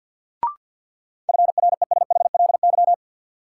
Morse code, heard as a keyed tone, about 46 wpm, 700 Hz. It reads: FRESHLY
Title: Sets of 5 Words Encore - New Word Drill - 50wpm